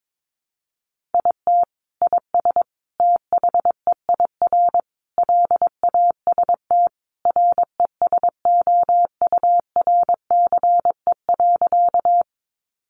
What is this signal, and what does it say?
Morse code, heard as a keyed tone, about 22 wpm, 700 Hz. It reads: IT IS THEIR LAST RESOURCE.
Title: Sentences from Top 1300 Words 22wpm